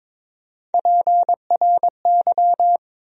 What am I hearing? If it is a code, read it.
Morse code, heard as a keyed tone, about 21 wpm, 700 Hz. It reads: PRY
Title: USA Call Sign Postfixes - 3 Letters 22wpm